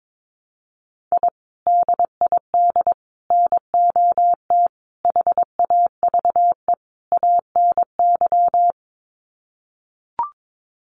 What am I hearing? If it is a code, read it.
Morse code, heard as a keyed tone, about 22 wpm, 700 Hz. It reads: I DID NOT HAVE ANY
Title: Sets of 5 Words Encore 22wpm